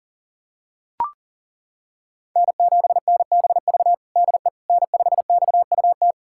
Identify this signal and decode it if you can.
Morse code, heard as a keyed tone, about 40 wpm, 700 Hz. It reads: N7NBV DE N5XUT
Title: USA Callsign DE Callsign - 5 Characters - 40wpm